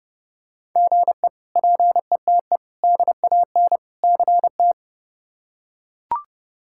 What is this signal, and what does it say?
Morse code, heard as a keyed tone, about 30 wpm, 700 Hz. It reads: GE PETE DAN CT
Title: K1USN SST QSO Practice - 30wpm